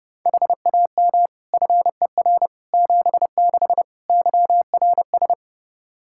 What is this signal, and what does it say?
Morse code, heard as a keyed tone, about 29 wpm, 700 Hz. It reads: HAM FER 76 YRS
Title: Practice QSO - Exchange 3 - More Info - 30wpm